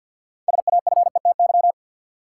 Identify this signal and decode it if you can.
Morse code, heard as a keyed tone, about 52 wpm, 700 Hz. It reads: SAFETY
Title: Sentences from Top 700 Words - New Word Drill - 50wpm